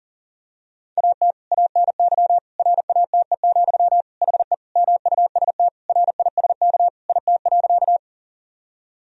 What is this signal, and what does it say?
Morse code, heard as a keyed tone, about 40 wpm, 700 Hz. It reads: AT ANY RATE, HE MUST RISK IT.
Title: Sentences from Top 1000 Words - Review Part 4 - 40wpm